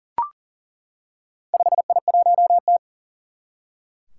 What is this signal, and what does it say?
Morse code, heard as a keyed tone, about 40 wpm, 700 Hz. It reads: 5I1T
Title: Four Challenging Letter-Number Combinations 40wpm